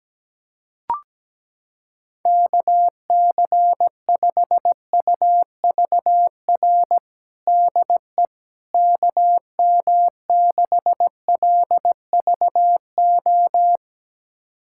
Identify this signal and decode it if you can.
Morse code, heard as a keyed tone, about 17 wpm, 700 Hz. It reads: KC5UVR DE KM6LVO
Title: USA Callsign DE Callsign 17wpm